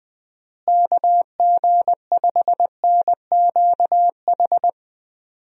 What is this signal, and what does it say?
Morse code, heard as a keyed tone, about 20 wpm, 700 Hz. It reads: KG5NQH
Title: USA Call Signs - 6 Characters - Repeated 2x - 20wpm